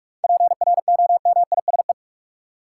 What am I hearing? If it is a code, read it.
Morse code, heard as a keyed tone, about 41 wpm, 700 Hz. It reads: PROMISE
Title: Seven Letter Words - 45wpm